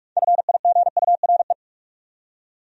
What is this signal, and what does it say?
Morse code, heard as a keyed tone, about 41 wpm, 700 Hz. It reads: FIGURE